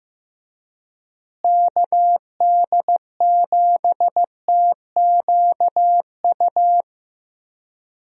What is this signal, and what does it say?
Morse code, heard as a keyed tone, about 15 wpm, 700 Hz. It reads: KD7TQU
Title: USA Call Signs - 6 Characters - Repeated 3x - 15wpm